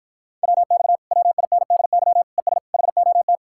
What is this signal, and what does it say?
Morse code, heard as a keyed tone, about 52 wpm, 700 Hz. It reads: WX WINDY ES HOT